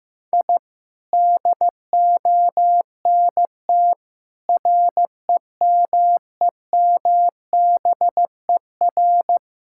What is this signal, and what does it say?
Morse code, heard as a keyed tone, about 15 wpm, 700 Hz. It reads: I DONT REMEMBER
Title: Sentences from Top 1000 Words - Review Part 6 - 15wpm